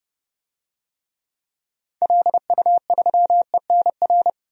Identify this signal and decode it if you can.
Morse code, heard as a keyed tone, about 30 wpm, 700 Hz. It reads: LU3ENR
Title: DXCC-100 Call Signs - 6 Characters - 30wpm